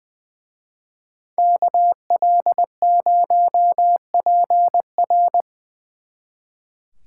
Morse code, heard as a keyed tone, about 24 wpm, 700 Hz. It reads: KL0PR
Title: Speed Racing - USA 2x2 Call Signs 20wpm